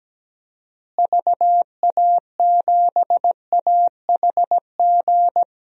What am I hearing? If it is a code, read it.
Morse code, heard as a keyed tone, about 17 wpm, 700 Hz. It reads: VA7AHG
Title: DXCC-100 Call Signs - 6 Characters - Repeated 2x - 17wpm